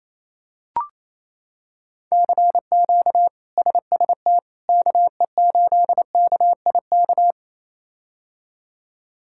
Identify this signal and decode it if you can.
Morse code, heard as a keyed tone, about 28 wpm, 700 Hz. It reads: CQ SST KE8KIK
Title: K1USN SST QSO Practice - 28wpm